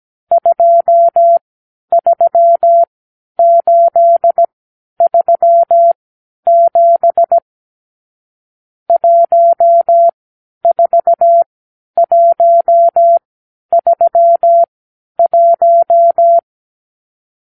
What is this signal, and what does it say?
Morse code, heard as a keyed tone, about 17 wpm, spaced as for 10 wpm, 680 Hz. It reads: 23837 14131
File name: k7qo / Lesson 54 (9)